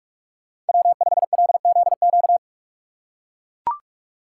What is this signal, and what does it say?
Morse code, heard as a keyed tone, about 44 wpm, 700 Hz. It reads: W5LZQ